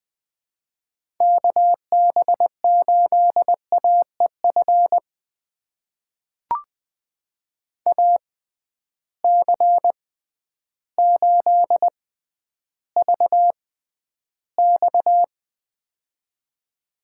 Morse code, heard as a keyed tone, about 20 wpm, spaced as for 11 wpm, 700 Hz. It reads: KB8AEF AC8VX
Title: USA Call Signs - 6x Character Spacing - 20wpm